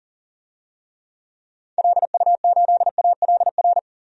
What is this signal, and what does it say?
Morse code, heard as a keyed tone, about 40 wpm, 700 Hz. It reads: LU8ALR
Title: DXCC-100 Call Signs - 6 Characters - 40wpm